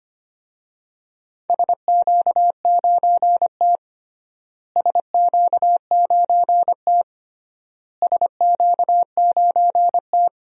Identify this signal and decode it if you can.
Morse code, heard as a keyed tone, about 25 wpm, 700 Hz. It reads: SQ9T SQ9T SQ9T
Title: DXCC-100 Call Signs - 4 Characters - Repeated 3x - 25wpm